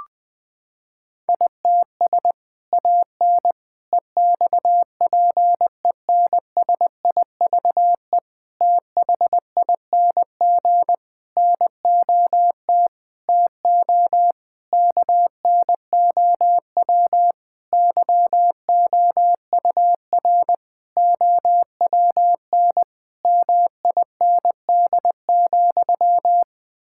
Morse code, heard as a keyed tone, about 20 wpm, 700 Hz. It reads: ITS AN EXPENSIVE THING NOT TO KNOW YOUR OWN MIND,
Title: Sentences from Top 1000 Words - Review Part 6 - 20wpm